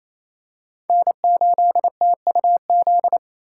Morse code, heard as a keyed tone, about 27 wpm, 700 Hz. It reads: N8TUZ
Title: USA Call Signs - 2x Character Spacing - 28wpm